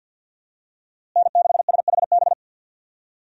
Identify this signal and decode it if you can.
Morse code, heard as a keyed tone, about 53 wpm, 700 Hz. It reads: N6SHB
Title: USA Call Signs - 5 Characters - Repeated 2x - 50wpm